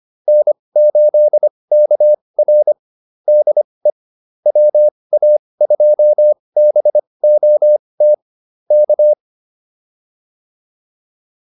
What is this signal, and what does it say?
Morse code, heard as a keyed tone, about 25 wpm, spaced as for 20 wpm, 600 Hz. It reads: N8KR DE WA2BOT K
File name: QSO206a 20